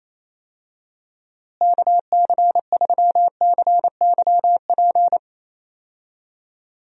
Morse code, heard as a keyed tone, about 28 wpm, 700 Hz. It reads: KC3CYP
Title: USA Call Signs - 6 Characters - Repeated 3x - 28wpm